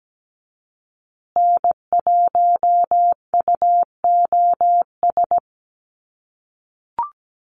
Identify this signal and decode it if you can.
Morse code, heard as a keyed tone, about 17 wpm, 700 Hz. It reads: N1UOS
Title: USA Call Signs Repeated 3x 17wpm